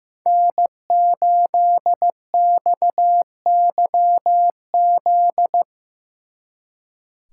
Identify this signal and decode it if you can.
Morse code, heard as a keyed tone, about 18 wpm, 700 Hz. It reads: N8XYZ
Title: Speed Racing - USA 1x3 Call Signs 15wpm